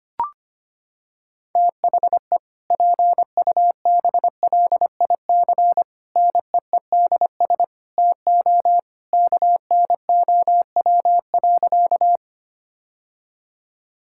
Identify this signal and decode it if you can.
Morse code, heard as a keyed tone, about 25 wpm, 700 Hz. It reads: THE PUBLIC NEEDS TO KNOW.
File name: practice / 25wpm / Sentences with 2-Word Phrases - 25wpm